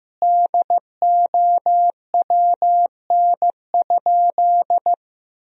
Morse code, heard as a keyed tone, about 15 wpm, 700 Hz. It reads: DOWN?
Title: Sentences from Top 1000 Words - Review Part 1 - 15wpm